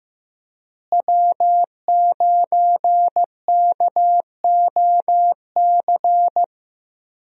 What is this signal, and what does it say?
Morse code, heard as a keyed tone, about 15 wpm, 700 Hz. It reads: W9KOC
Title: USA Call Signs - 5 Characters - Repeated 3x - 15wpm